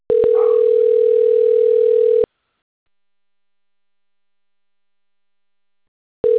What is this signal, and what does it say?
Ringback tone (North America), 440 + 480 Hz.